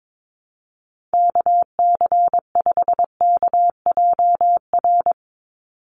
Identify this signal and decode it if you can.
Morse code, heard as a keyed tone, about 22 wpm, 700 Hz. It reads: KC5KJR
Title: USA Call Signs 22wpm